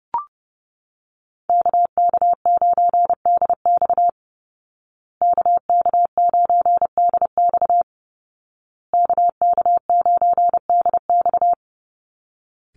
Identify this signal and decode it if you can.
Morse code, heard as a keyed tone, about 30 wpm, 700 Hz. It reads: KK9DX KK9DX KK9DX
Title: Call Signs Repeated 2x - 30wpm